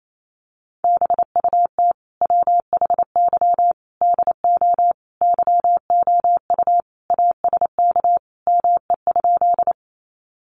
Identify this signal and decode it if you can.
Morse code, heard as a keyed tone, about 28 wpm, 700 Hz. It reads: BUT WHY DO YOU ASK ME?
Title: Sentences from Top 200 Words 28wpm